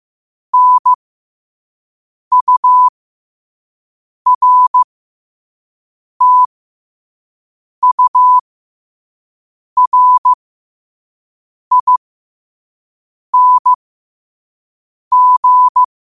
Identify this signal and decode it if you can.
Morse code, heard as a keyed tone, about 15 wpm, spaced as for 5 wpm, 1000 Hz. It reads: NURTURING
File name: qotdfiles / 5wpm